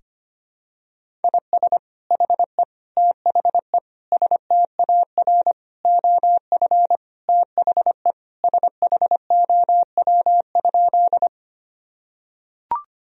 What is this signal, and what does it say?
Morse code, heard as a keyed tone, about 25 wpm, 700 Hz. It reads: IS HE THE STAR OF THE SHOW?
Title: Sentences from Top 1000 Words - Review Part 2 - 25wpm